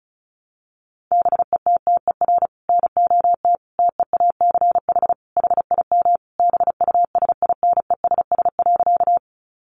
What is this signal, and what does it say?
Morse code, heard as a keyed tone, about 34 wpm, 700 Hz. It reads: BETTER NOT TEACH HIM BUSINESS.
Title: Sentences from Top 600 Words 35wpm